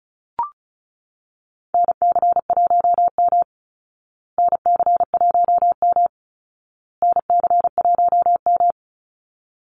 Morse code, heard as a keyed tone, about 34 wpm, 700 Hz. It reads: NC1M NC1M NC1M
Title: Call Signs Repeated 2x - 35wpm